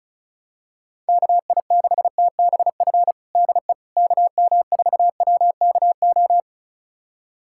Morse code, heard as a keyed tone, about 34 wpm, 700 Hz. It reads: KI6TBF DE KM4WKO
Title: USA Callsign DE Callsign 35wpm